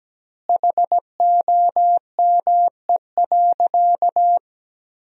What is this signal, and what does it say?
Morse code, heard as a keyed tone, about 17 wpm, 700 Hz. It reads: HOME.
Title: 2x Word-Spacing - Sentences from Top 500 Words - 17wpm